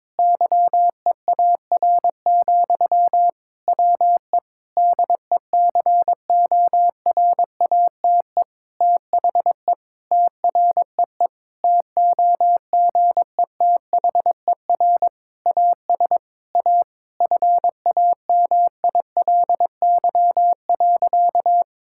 Morse code, heard as a keyed tone, about 22 wpm, 700 Hz. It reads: YEAR, WE DECORATE THE TREE TOGETHER AS A FAMILY.
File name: Difficult Sentences with Hints - 22wpm